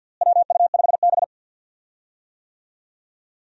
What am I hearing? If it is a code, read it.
Morse code, heard as a keyed tone, about 51 wpm, 700 Hz. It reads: WU5B
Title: USA Call Signs - 4 Characters - Repeated 2x - 50wpm